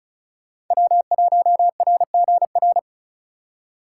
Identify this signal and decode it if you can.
Morse code, heard as a keyed tone, about 46 wpm, 700 Hz. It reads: W1RGR